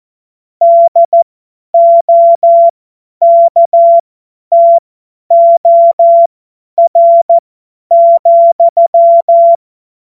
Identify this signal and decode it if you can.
Morse code, heard as a keyed tone, about 14 wpm, spaced as for 10 wpm, 680 Hz. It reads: DOKTOR,
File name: Morsekurs / Witze / Witz 24 Juni 2025 50 F70